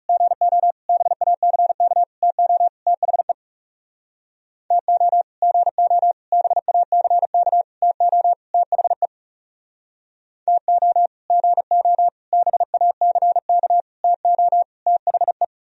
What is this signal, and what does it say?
Morse code, heard as a keyed tone, about 38 wpm, 700 Hz. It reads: GO BACK TO THE TO GO BACK TO THE TO GO BACK TO THE